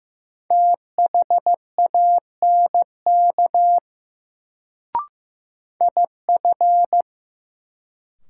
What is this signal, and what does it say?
Morse code, heard as a keyed tone, about 15 wpm, 700 Hz. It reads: THANK IF
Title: Sets of 2 Words - New Word Drill - 15wpm